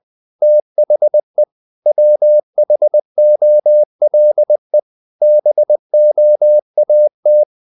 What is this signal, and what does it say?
Morse code, heard as a keyed tone, about 20 wpm, 600 Hz. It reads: THE WHOLE BOAT
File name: the old man and the sea / the old man and the sea part 1 20wpm